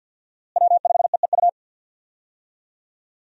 Morse code, heard as a keyed tone, about 50 wpm, 700 Hz. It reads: W5EEU
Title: USA Call Signs - 5 Characters - Repeated 2x - 50wpm